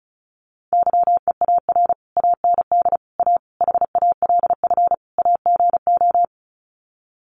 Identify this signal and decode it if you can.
Morse code, heard as a keyed tone, about 34 wpm, 700 Hz. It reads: YEAR AND A HALF AGO
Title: Sets of 5 Words Encore 35wpm